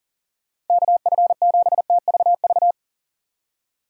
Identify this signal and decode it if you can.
Morse code, heard as a keyed tone, about 40 wpm, 700 Hz. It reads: KF7TVV
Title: USA Call Signs - 6 Characters - 8x Character Spacing - 40wpm